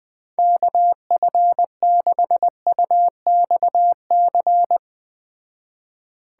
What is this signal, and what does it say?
Morse code, heard as a keyed tone, about 20 wpm, 700 Hz. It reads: KF6UXC